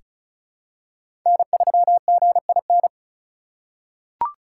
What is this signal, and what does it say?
Morse code, heard as a keyed tone, about 34 wpm, 690 Hz. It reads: N3GIN
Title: USA Call Signs - 5 Characters - 2x Character Spacing - 35wpm